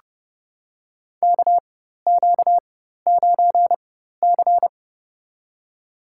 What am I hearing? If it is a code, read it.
Morse code, heard as a keyed tone, about 30 wpm, spaced as for 14 wpm, 700 Hz. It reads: KQ9C
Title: USA Call Signs - 4 Characters - 4x Character Spacing - 30wpm